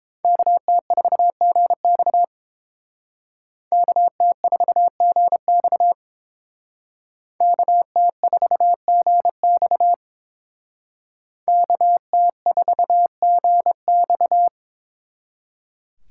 Morse code, heard as a keyed tone, about 29 wpm, 700 Hz. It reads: KT4GX KT4GX KT4GX KT4GX